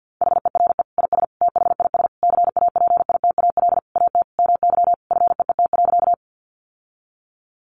Morse code, heard as a keyed tone, about 50 wpm, 700 Hz. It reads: HERE IS THIS CAPITAL AT MY FEET.
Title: Sentences from Top 1000 Words 50wpm